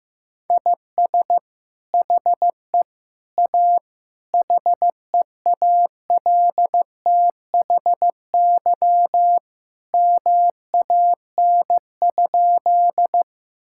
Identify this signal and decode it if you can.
Morse code, heard as a keyed tone, about 15 wpm, 700 Hz. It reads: IS HE A HEALTHY MAN?
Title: Sentences from Top 1000 Words - Review Part 5 - 15wpm